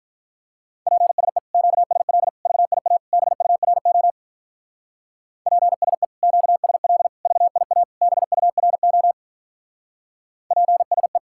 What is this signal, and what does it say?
Morse code, heard as a keyed tone, about 48 wpm, 700 Hz. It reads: PSE QSL VIA BURO PSE QSL VIA BURO PSE